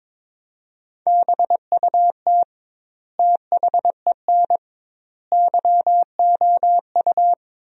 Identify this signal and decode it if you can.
Morse code, heard as a keyed tone, about 22 wpm, 700 Hz. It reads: BUT THEN YOU
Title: 2x Word-Spacing - Sentences from Top 300 Words - 22wpm